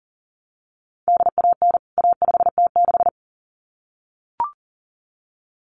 Morse code, heard as a keyed tone, about 40 wpm, 700 Hz. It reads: DAN A5T6
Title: CWT Exchange 40wpm